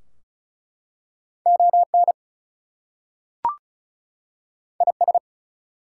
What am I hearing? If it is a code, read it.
Morse code, heard as a keyed tone, about 36 wpm, 700 Hz. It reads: ON IS
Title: Two Letter Words - 35wpm